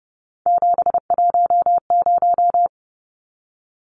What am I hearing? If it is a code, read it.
Morse code, heard as a keyed tone, about 30 wpm, 700 Hz. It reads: 710
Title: Numbers - 3-Digit 30wpm